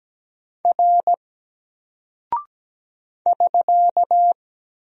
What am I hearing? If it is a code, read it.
Morse code, heard as a keyed tone, about 17 wpm, 700 Hz. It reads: R <SK>
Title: Single Character (Letter:Number:Punctuation:Prosign) 17wpm